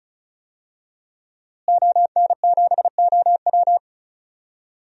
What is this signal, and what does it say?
Morse code, heard as a keyed tone, about 36 wpm, 700 Hz. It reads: ON7OW
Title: DXCC-100 Call Signs - 5 Characters - Repeated 3x - 35wpm